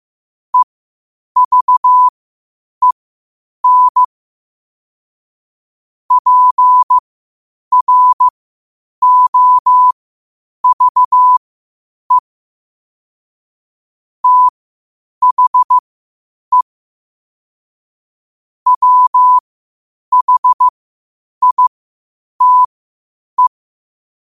Morse code, heard as a keyed tone, about 15 wpm, spaced as for 8 wpm, 1000 Hz. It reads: EVEN PROVE THE WHITE
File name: qotdfiles / 8wpm